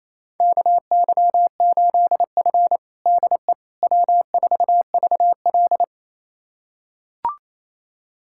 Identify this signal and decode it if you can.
Morse code, heard as a keyed tone, about 28 wpm, 700 Hz. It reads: KY8F DE W4VL
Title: USA Callsign DE Callsign - 4 Characters - 28wpm